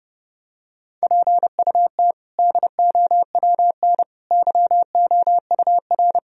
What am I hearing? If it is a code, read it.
Morse code, heard as a keyed tone, about 30 wpm, 700 Hz. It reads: PUT DOWN YOUR
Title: Sentences from Top 1000 Words - Review Part 1 - 30wpm